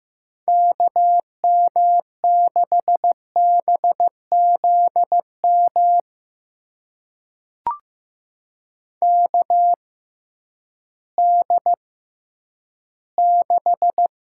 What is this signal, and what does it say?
Morse code, heard as a keyed tone, about 15 wpm, spaced as for 10 wpm, 700 Hz. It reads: KM6BZM KD6
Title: USA Call Signs - 6 Characters - 6x Character Spacing - 15wpm